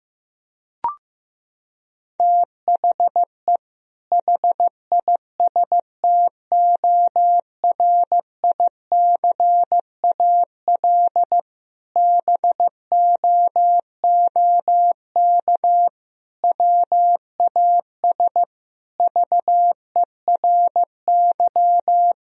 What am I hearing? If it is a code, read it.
Morse code, heard as a keyed tone, about 15 wpm, 700 Hz. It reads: THE HISTORICAL BOOK WAS VERY
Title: Sentences from Top 1000 Words - Review Part 4 - 15wpm